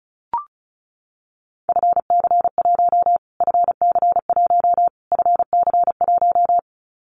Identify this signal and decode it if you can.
Morse code, heard as a keyed tone, about 34 wpm, 700 Hz. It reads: FC1 FC1 FC1
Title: Callsign Prefixes for Top 25 DXCC - Repeated 3x - 35wpm